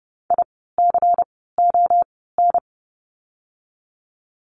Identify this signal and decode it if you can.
Morse code, heard as a keyed tone, about 30 wpm, 700 Hz. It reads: I C O N
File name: Lesson 16f - Letter c - Words 30wpm